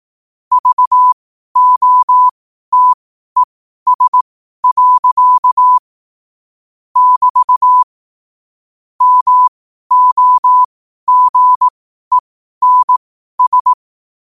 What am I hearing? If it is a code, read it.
Morse code, heard as a keyed tone, about 18 wpm, spaced as for 12 wpm, 1000 Hz. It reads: VOTES. <BT> MOGENS
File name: qotdfiles / 12wpm